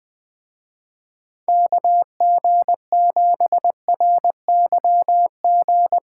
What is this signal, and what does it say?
Morse code, heard as a keyed tone, about 20 wpm, 700 Hz. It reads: KG7RYG